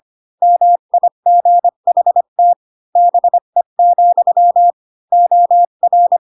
Morse code, heard as a keyed tone, about 25 wpm, 700 Hz. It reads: MIGHT BE, OR
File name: Pinocchio / Pinocchio ch5 20 25 ik1hge